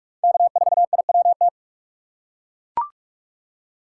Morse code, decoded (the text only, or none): K4IWT